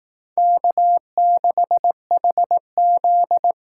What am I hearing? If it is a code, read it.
Morse code, heard as a keyed tone, about 18 wpm, 700 Hz. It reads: K6HZ
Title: Speed Racing - USA 1x2 Call Signs 15wpm